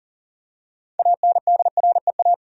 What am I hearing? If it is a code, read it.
Morse code, heard as a keyed tone, about 40 wpm, 700 Hz. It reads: ANDREA
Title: USA Common Names - 40wpm